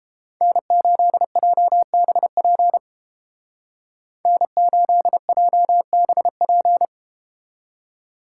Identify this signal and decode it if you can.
Morse code, heard as a keyed tone, about 32 wpm, 700 Hz. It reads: N8JBP N8JBP